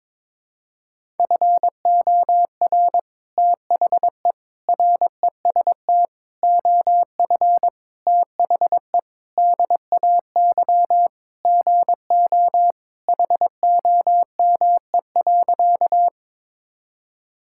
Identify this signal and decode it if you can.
Morse code, heard as a keyed tone, about 22 wpm, 700 Hz. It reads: FOR THE REST OF THE DAY GO HOME.
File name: practice / 22wpm / Sentences with 3-Word Phrases - 22wpm